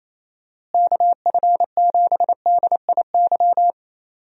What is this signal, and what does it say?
Morse code, heard as a keyed tone, about 28 wpm, 700 Hz. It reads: KF7DIY
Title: USA Call Signs - 6 Characters - Repeated 2x - 28wpm